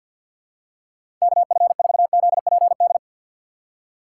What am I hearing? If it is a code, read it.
Morse code, heard as a keyed tone, about 53 wpm, 700 Hz. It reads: KF4ZPD
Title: USA Call Signs - 6x Character Spacing - 50wpm